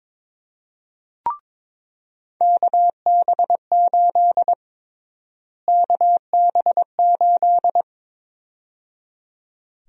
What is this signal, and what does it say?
Morse code, heard as a keyed tone, about 22 wpm, 700 Hz. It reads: KB8 KB8
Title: USA Call Sign Prefixes - Repeated 2x - 22wpm